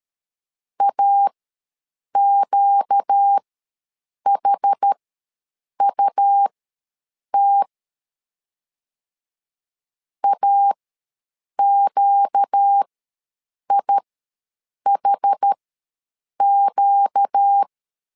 Morse code, heard as a keyed tone, about 13 wpm, spaced as for 7 wpm, 800 Hz. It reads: AQHUT AQIHQ
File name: LEZIONI / 4 Q ws 12wpm 1912